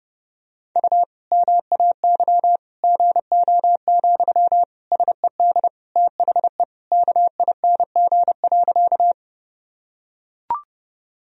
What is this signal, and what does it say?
Morse code, heard as a keyed tone, about 30 wpm, 700 Hz. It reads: U MAY GO, SED THE KING.
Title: Sentences from Top 400 Words and Abbreviations 30wpm